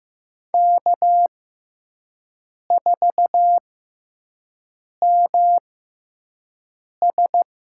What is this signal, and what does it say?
Morse code, heard as a keyed tone, about 15 wpm, spaced as for 5 wpm, 700 Hz. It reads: K4MS
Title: USA Call Signs - 5 Characters - 6x Character Spacing - 15wpm